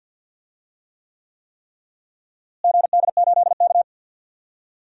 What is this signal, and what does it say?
Morse code, heard as a keyed tone, about 51 wpm, 700 Hz. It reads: GD8K
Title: DXCC-100 Call Signs - 4 Characters - Repeated 3x - 50wpm